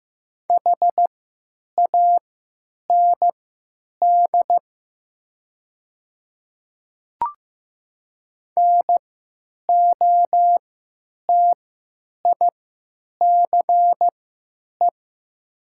Morse code, heard as a keyed tone, about 15 wpm, spaced as for 8 wpm, 700 Hz. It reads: HAND NOTICE
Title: 3x Character Spacing - Top 500 Words - 15wpm